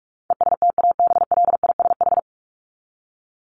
Morse code, heard as a keyed tone, about 44 wpm, 700 Hz. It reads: ESTABLISH
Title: ED Suffix - Top 100 Words with Base Words 45wpm